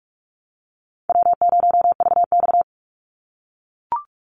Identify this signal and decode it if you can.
Morse code, heard as a keyed tone, about 44 wpm, 700 Hz. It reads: W0VX
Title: Easy Call Signs - 45wpm